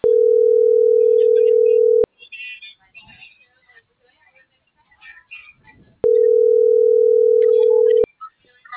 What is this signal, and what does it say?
Ringback tone (North America), 440 + 480 Hz.